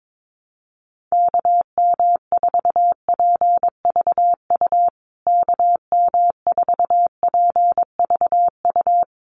Morse code, heard as a keyed tone, about 22 wpm, 700 Hz. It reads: KM4PVU KM4PVU
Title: USA Call Signs Repeated 3x 22wpm